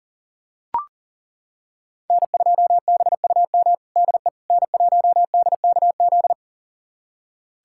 Morse code, heard as a keyed tone, about 40 wpm, 700 Hz. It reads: N2BUM DE N1DKZ
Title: USA Callsign DE Callsign - 5 Characters - 40wpm